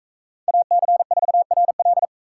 Morse code, heard as a keyed tone, about 41 wpm, 700 Hz. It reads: AC4RL